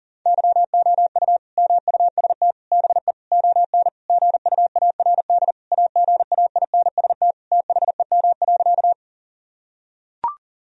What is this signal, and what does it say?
Morse code, heard as a keyed tone, about 37 wpm, 700 Hz. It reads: YOU MUST BE ON GUARD AGAINST THEM.
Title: Sentences from Top 1000 Words - Review Part 4 - 40wpm